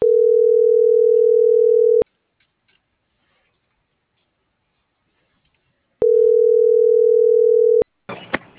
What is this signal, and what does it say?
Ringback tone (North America), 440 + 480 Hz.